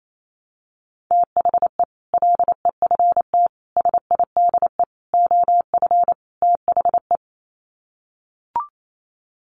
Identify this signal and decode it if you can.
Morse code, heard as a keyed tone, about 28 wpm, 700 Hz. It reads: THE LEFT SIDE OF THE
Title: Sets of 5 Words Encore 28wpm